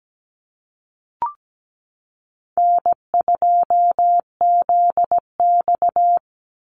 Morse code, heard as a keyed tone, about 17 wpm, 700 Hz. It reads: N2ZX
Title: Easy Call Signs Repeated 2x - 17wpm